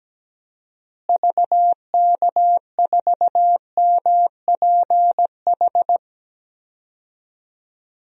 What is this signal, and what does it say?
Morse code, heard as a keyed tone, about 17 wpm, 700 Hz. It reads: VK4MPH
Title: DXCC-100 Call Signs - 6 Characters - Repeated 3x - 17wpm